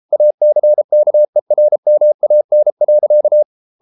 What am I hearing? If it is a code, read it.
Morse code, heard as a keyed tone, about 33 wpm, 600 Hz. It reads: ACKERMAN.